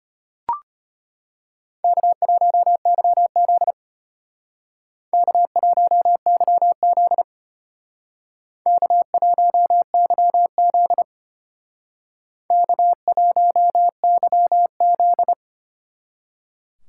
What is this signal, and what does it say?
Morse code, heard as a keyed tone, about 32 wpm, 700 Hz. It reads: K1YZ K1YZ K1YZ K1YZ